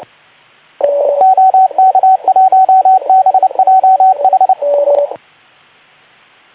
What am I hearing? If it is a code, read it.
Morse code, heard as a keyed tone, about 28 wpm, 750 Hz, with radio noise. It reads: OK1BJH
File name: Archiv / CUC189 / ok2zcq cuc 189